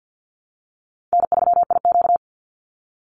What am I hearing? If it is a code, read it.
Morse code, heard as a keyed tone, about 48 wpm, 700 Hz. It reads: N3IQ
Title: Call Signs Repeated 3x - 50wpm